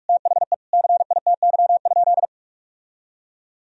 Morse code, heard as a keyed tone, about 46 wpm, 700 Hz. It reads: THE CITY?
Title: Sentences from Top 1000 Words - Review Part 3 - 45wpm